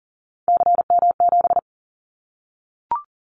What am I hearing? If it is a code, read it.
Morse code, heard as a keyed tone, about 40 wpm, 700 Hz. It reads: CM7
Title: Callsign Prefixes for Top 100 DXCC - 40wpm